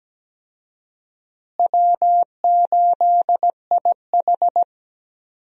Morse code, heard as a keyed tone, about 17 wpm, 700 Hz. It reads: W8IH
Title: USA Call Signs - 4 Characters - Repeated 2x - 17wpm